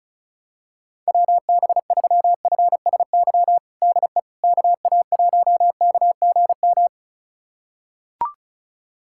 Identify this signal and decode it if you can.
Morse code, heard as a keyed tone, about 36 wpm, 700 Hz. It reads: WB3FSY DE KA1KGM
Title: USA Callsign DE Callsign - 6 Characters - 35wpm